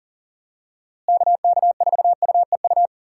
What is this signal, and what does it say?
Morse code, heard as a keyed tone, about 40 wpm, 700 Hz. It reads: KK4UEU
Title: USA Call Signs - 6 Characters - Repeated 2x - 40wpm